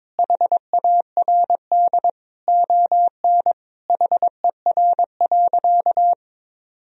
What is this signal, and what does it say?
Morse code, heard as a keyed tone, about 22 wpm, 700 Hz. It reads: HARD ON HER.
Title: Sentences from Top 1000 Words - Review Part 1 - 22wpm